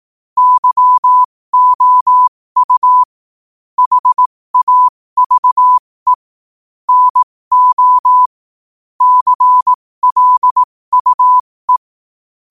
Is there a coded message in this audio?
Yes: Morse code from 0.4 to 11.8 s, heard as a keyed tone, about 18 wpm, spaced as for 15 wpm, 1000 Hz. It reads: YOU HAVE NO CLUE